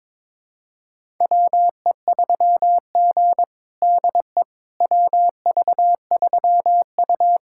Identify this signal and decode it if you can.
Morse code, heard as a keyed tone, about 21 wpm, 700 Hz. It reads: WE3G DE WV3U
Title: USA Callsign DE Callsign - 4 Characters - 22wpm